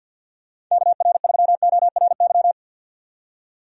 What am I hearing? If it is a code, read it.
Morse code, heard as a keyed tone, about 46 wpm, 700 Hz. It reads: KR3ORY
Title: USA Call Signs - 6 Characters - 50wpm